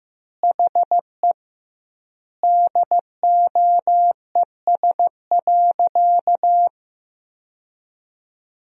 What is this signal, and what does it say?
Morse code, heard as a keyed tone, about 15 wpm, 700 Hz. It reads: HE DOES.
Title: 2x Word-Spacing - Sentences from Top 500 Words - 15wpm